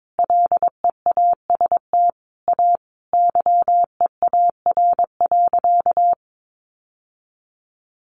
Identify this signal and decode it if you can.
Morse code, heard as a keyed tone, about 22 wpm, 700 Hz. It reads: LEAST A YEAR.
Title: Sentences from Top 700 Words 22wpm